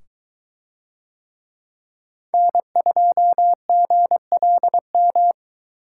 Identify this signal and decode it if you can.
Morse code, heard as a keyed tone, about 21 wpm, 700 Hz. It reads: N2GLM